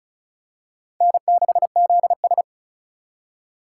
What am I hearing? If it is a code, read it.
Morse code, heard as a keyed tone, about 36 wpm, 700 Hz. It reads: N6ZS
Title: USA Call Signs - 4 Characters - 3x Character Spacing - 35wpm